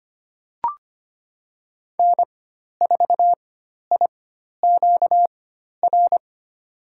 Morse code, heard as a keyed tone, about 25 wpm, spaced as for 12 wpm, 700 Hz. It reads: N4IQR
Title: USA Call Signs - 4x Character Spacing - 25wpm